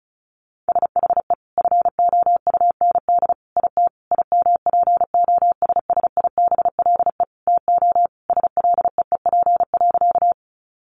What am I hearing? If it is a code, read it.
Morse code, heard as a keyed tone, about 34 wpm, 700 Hz. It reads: SHE FOUND IT IMPOSSIBLE TO SLEEP.
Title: Sentences from Top 1100 Words 35wpm